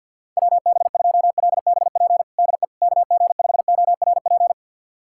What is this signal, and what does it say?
Morse code, heard as a keyed tone, about 50 wpm, 690 Hz. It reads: WB2LBP DE KG5ORP
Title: USA Callsign DE Callsign - 6 Characters - 50wpm